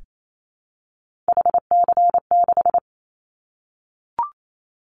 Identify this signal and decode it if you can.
Morse code, heard as a keyed tone, about 28 wpm, 700 Hz. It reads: HC6
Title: Callsign Prefixes for Top 100 DXCC - 28wpm